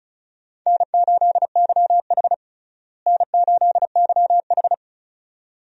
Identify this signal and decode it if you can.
Morse code, heard as a keyed tone, about 36 wpm, 700 Hz. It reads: N8YH N8YH
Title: USA Call Signs - 4 Characters - Repeated 2x - 35wpm